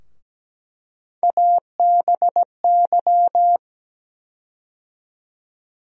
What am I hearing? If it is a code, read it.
Morse code, heard as a keyed tone, about 17 wpm, 700 Hz. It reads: ABY
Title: USA Call Sign Postfixes - 3 Letters 17wpm